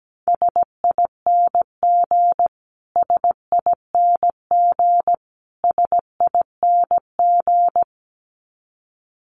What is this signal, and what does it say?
Morse code, heard as a keyed tone, about 17 wpm, 700 Hz. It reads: SING SING SING
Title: Top 500 Words Repeated 17wpm